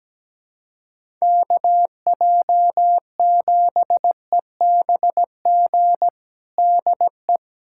Morse code, heard as a keyed tone, about 17 wpm, 700 Hz. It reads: KJ7EBG DE
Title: USA Callsign DE Callsign - 6 Characters - 17wpm